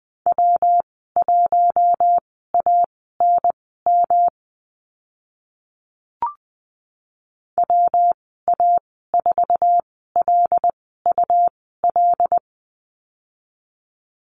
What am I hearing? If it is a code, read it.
Morse code, heard as a keyed tone, about 20 wpm, spaced as for 14 wpm, 700 Hz. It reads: W1ANM WA4LUL
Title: 128 Lesson 21g - Number 6 - Call Signs 20wpm